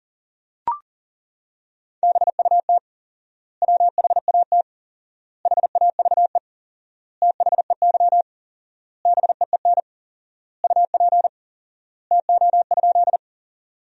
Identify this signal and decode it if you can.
Morse code, heard as a keyed tone, about 40 wpm, 700 Hz. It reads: BUT WHAT HAVE THEY BEEN UP TO?
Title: 4x Word-Spacing - Sentences from Top 100 Words - 40wpm